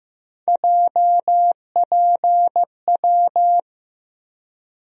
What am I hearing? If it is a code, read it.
Morse code, heard as a keyed tone, about 15 wpm, 700 Hz. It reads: JPW
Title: USA Call Sign Postfixes - 3 Letters 15wpm